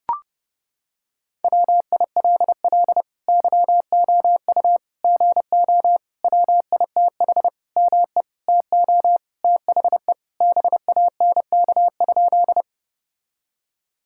Morse code, heard as a keyed tone, about 32 wpm, 700 Hz. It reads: WILL YOU GO WITH ME TO THE BANK?
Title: Sentences from Top 1000 Words - Review Part 6 - 30wpm